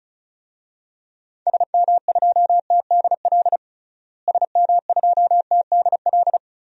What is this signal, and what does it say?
Morse code, heard as a keyed tone, about 36 wpm, 700 Hz. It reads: SM2TDL SM2TDL
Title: DXCC-100 Call Signs - 6 Characters - Repeated 3x - 35wpm